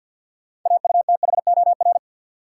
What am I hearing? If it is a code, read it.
Morse code, heard as a keyed tone, about 46 wpm, 700 Hz. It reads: AUTHOR
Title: Sentences from Top 1200 Words - New Word Drill Alt - 50wpm